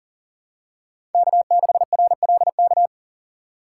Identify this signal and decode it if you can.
Morse code, heard as a keyed tone, about 40 wpm, 700 Hz. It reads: K6RLK
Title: USA Call Signs - 5 Characters - 4x Character Spacing - 40wpm